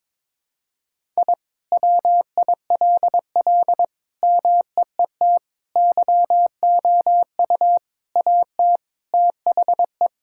Morse code, heard as a keyed tone, about 22 wpm, 700 Hz. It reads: I WILL MEET YOU AT THE
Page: morse-code-ninja-practice-22wpm